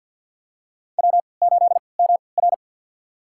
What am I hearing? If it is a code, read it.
Morse code, heard as a keyed tone, about 46 wpm, spaced as for 27 wpm, 700 Hz. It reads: W8MR
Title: USA Call Signs - 3x Character Spacing - 50wpm